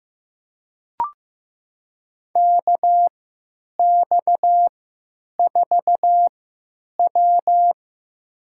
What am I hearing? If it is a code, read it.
Morse code, heard as a keyed tone, about 15 wpm, spaced as for 9 wpm, 700 Hz. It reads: KX4W
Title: USA Call Signs - 3x Character Spacing - 15wpm